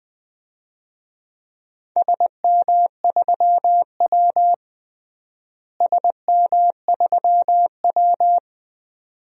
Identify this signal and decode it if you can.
Morse code, heard as a keyed tone, about 20 wpm, 700 Hz. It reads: SM3W SM3W
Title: DXCC-100 Call Signs - 4 Characters - Repeated 2x - 20wpm